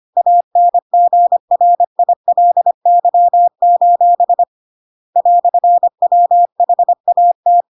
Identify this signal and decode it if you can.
Morse code, heard as a keyed tone, about 25 wpm, 700 Hz. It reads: ANGRILY: "WHAT